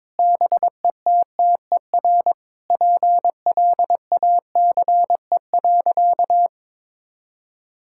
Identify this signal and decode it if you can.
Morse code, heard as a keyed tone, about 21 wpm, 700 Hz. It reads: BETTER PLACE.
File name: Sentences with 7-Word Phrases - 22wpm